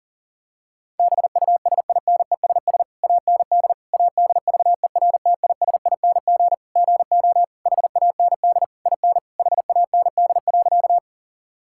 Morse code, heard as a keyed tone, about 40 wpm, 700 Hz. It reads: BUSINESS AND ADVERTISING GO HAND IN HAND.
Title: Sentences from Top 1000 Words - Review Part 6 - 40wpm